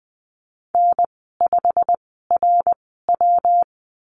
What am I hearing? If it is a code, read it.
Morse code, heard as a keyed tone, about 20 wpm, spaced as for 15 wpm, 700 Hz. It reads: N5RW
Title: 128 Lesson 21g - Number 6 - Call Signs 20wpm